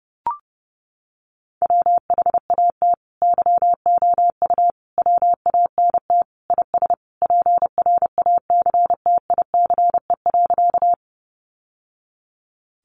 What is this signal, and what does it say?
Morse code, heard as a keyed tone, about 30 wpm, 700 Hz. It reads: WHAT YOU WANT IS PRACTICE.
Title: Sentences from Top 700 Words 30wpm